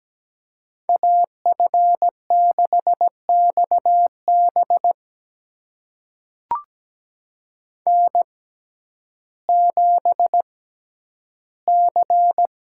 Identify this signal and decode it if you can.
Morse code, heard as a keyed tone, about 17 wpm, spaced as for 10 wpm, 700 Hz. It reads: AF6XB N7C
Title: USA Call Signs - 6x Character Spacing - 17wpm